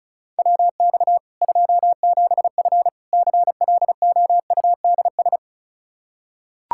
Morse code, heard as a keyed tone, about 36 wpm, 700 Hz. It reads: WX 27F CLOUDS
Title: Practice QSO - Exchange 2 - Info - 35wpm